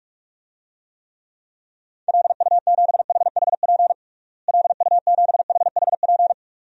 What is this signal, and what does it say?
Morse code, heard as a keyed tone, about 47 wpm, 700 Hz. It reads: LU7HHP LU7HHP
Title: DXCC-100 Call Signs - 6 Characters - Repeated 2x - 45wpm